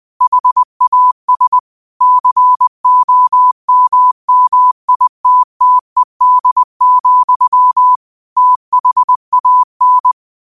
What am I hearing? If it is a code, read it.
Morse code, heard as a keyed tone, about 20 wpm, 1000 Hz. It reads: HAS COMMITTED, THAN
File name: 20wpm